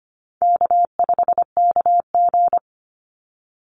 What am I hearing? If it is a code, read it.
Morse code, heard as a keyed tone, about 25 wpm, 700 Hz. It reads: K5KG
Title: Easy Call Signs Repeated 2x - 25wpm